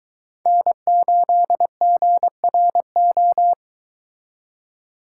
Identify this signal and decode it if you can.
Morse code, heard as a keyed tone, about 23 wpm, 700 Hz. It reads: N8GRO